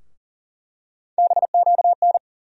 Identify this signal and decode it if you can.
Morse code, heard as a keyed tone, about 40 wpm, 700 Hz. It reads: BQN
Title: USA Call Sign Postfixes - 3 Letters 40wpm